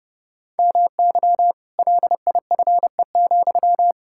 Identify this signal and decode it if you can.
Morse code, heard as a keyed tone, about 30 wpm, 700 Hz. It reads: MY LIFE,
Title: Sentences from Top 1600 Words 30wpm